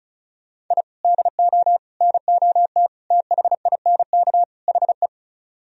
Morse code, heard as a keyed tone, about 42 wpm, 700 Hz. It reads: I DO NOT THINK HE
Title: Speed-Racing - Sets of 5 Words 35wpm